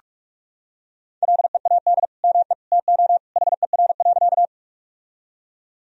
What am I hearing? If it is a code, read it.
Morse code, heard as a keyed tone, about 45 wpm, 700 Hz. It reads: LEAD ME TO HER.
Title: Sentences from Top 1000 Words - Review Part 2 - 45wpm